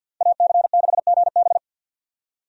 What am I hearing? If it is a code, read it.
Morse code, heard as a keyed tone, about 47 wpm, 700 Hz. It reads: AC6GB
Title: USA Call Signs - 5 Characters - 50wpm